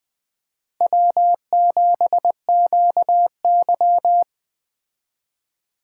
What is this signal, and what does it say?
Morse code, heard as a keyed tone, about 19 wpm, 700 Hz. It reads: W7QY